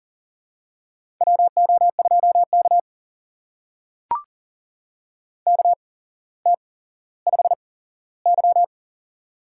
Morse code, heard as a keyed tone, about 38 wpm, spaced as for 17 wpm, 700 Hz. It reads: WO2K KT5Y
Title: USA Call Signs - 8x Character Spacing - 40wpm